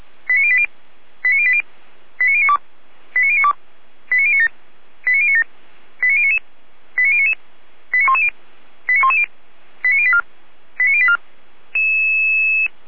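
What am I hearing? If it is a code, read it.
Five-tone selective call (ZVEI), heard as quick runs of five tones, tones of about 70 ms. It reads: 89080 89080 89092 89092 89097 89097 89099 89099 89110 89110 89084 89084
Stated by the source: ZVEI five-tone selective call received with an SDR, from the Feuerwehr Orlamünde fire brigade audio server.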